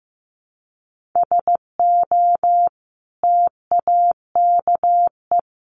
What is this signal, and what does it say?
Morse code, heard as a keyed tone, about 14 wpm, 700 Hz. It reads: SO TAKE
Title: Sentences from Top 1500 Words 15wpm